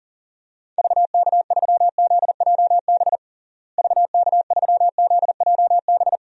Morse code, heard as a keyed tone, about 40 wpm, 700 Hz. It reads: VK3ZJB VK3ZJB
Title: DXCC-100 Call Signs - 6 Characters - Repeated 2x - 40wpm